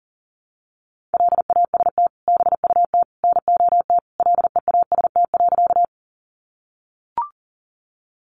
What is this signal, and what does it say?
Morse code, heard as a keyed tone, about 40 wpm, 700 Hz. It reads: LAST BUT NOT LEAST.
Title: Sentences from Top 700 Words 40wpm